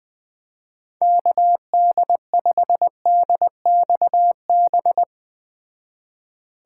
Morse code, heard as a keyed tone, about 20 wpm, 700 Hz. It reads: KD5DXB